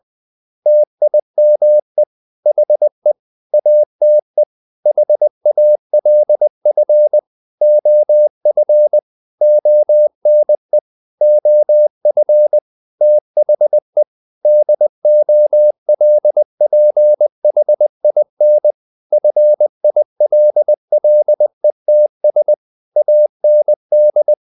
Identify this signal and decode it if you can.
Morse code, heard as a keyed tone, about 20 wpm, 600 Hz. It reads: TIME HE ATE HALF OF ONE OF THE DOLPHIN FILLETS AND
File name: the old man and the sea part 2 20wpm